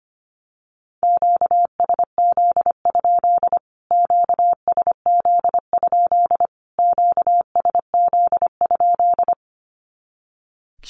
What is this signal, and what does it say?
Morse code, heard as a keyed tone, about 25 wpm, 700 Hz. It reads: QSZ? QSZ? QSZ?
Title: Q-Codes Repeated 3x 25wpm